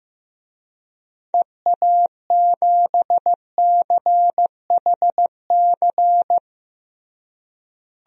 Morse code, heard as a keyed tone, about 15 wpm, 700 Hz. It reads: EA7CHC
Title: DXCC-100 Call Signs - 6 Characters - Repeated 3x - 15wpm